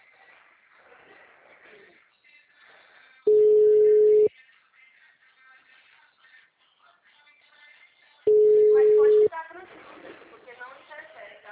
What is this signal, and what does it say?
Ringback tone (Europe, 425 Hz), with speech.
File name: out- -2006- - - .9124